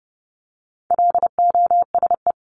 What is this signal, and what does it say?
Morse code, heard as a keyed tone, about 30 wpm, 700 Hz. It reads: LOSE
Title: Top 1001-1250 Words 30wpm